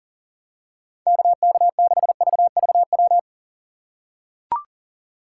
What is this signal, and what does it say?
Morse code, heard as a keyed tone, about 40 wpm, 700 Hz. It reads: KK6VVW
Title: USA Call Signs - 6 Characters - 4x Character Spacing - 40wpm